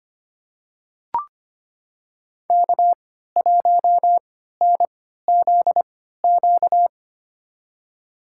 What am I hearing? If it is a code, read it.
Morse code, heard as a keyed tone, about 25 wpm, spaced as for 14 wpm, 700 Hz. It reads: K1NZQ
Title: USA Call Signs - 3x Character Spacing - 25wpm